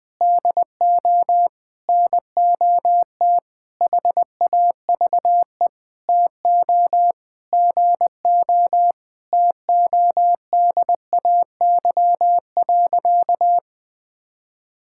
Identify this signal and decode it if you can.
Morse code, heard as a keyed tone, about 20 wpm, 700 Hz. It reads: DO NOT HAVE TO GO TODAY.
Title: Sentences with 5-Word Phrases - Spoken Beginning - 20wpm